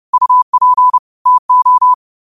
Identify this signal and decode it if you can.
Morse code, heard as a keyed tone, about 30 wpm, 1000 Hz. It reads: UP TO
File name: qotdfiles / 30wpm